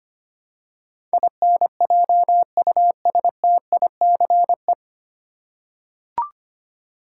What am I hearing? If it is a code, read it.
Morse code, heard as a keyed tone, about 25 wpm, 700 Hz. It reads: INJUSTICE
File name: Nine Letter Words - 25wpm